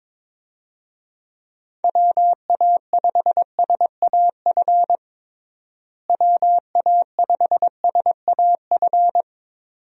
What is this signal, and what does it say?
Morse code, heard as a keyed tone, about 22 wpm, 700 Hz. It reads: WA5SAF WA5SAF
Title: USA Call Signs - 6 Characters - Repeated 2x - 22wpm